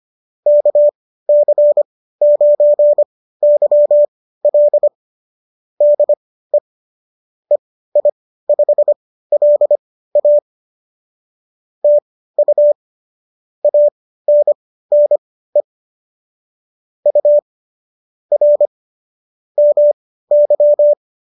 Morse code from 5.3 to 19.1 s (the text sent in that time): DE EI5LA TU ANNE U R